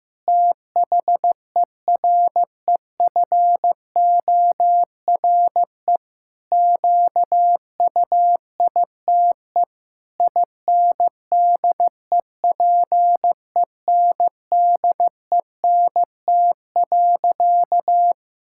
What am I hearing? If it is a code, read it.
Morse code, heard as a keyed tone, about 16 wpm, 700 Hz. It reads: THEREFORE QUITE INDEPENDENT.
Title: Sentences from Top 1000 Words - Review Part 6 - 15wpm